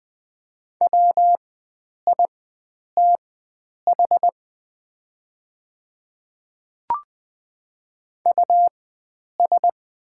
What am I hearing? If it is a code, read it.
Morse code, heard as a keyed tone, about 21 wpm, spaced as for 9 wpm, 700 Hz. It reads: WITH US
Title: 4x Character Spacing - Top 300 Words - 20wpm